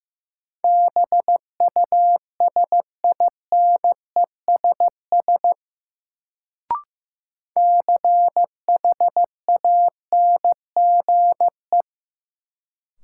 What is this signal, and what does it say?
Morse code, heard as a keyed tone, about 15 wpm, 700 Hz. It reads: BUSINESS CHANGE
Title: Sentences from Top 600 Words - New Word Drill - 15wpm